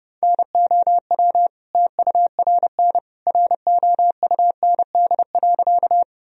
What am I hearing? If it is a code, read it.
Morse code, heard as a keyed tone, about 32 wpm, 700 Hz. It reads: NOW TURN ROUND.
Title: Sentences from Top 1000 Words - Review Part 1 - 30wpm